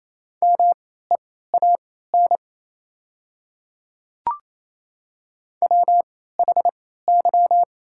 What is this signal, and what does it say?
Morse code, heard as a keyed tone, about 27 wpm, spaced as for 16 wpm, 690 Hz. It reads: MEAN WHY
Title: 3x Character Spacing - Top 200 Words - 28wpm